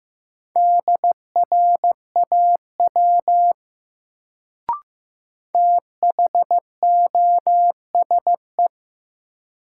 Morse code, heard as a keyed tone, about 15 wpm, 700 Hz. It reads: DRAW THOSE
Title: Sentences from Top 300 Words - New Word Drill - 15wpm